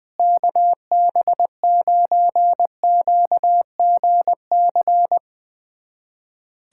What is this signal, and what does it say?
Morse code, heard as a keyed tone, about 21 wpm, 700 Hz. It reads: KB9QGC
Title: USA Callsign DE Callsign 20wpm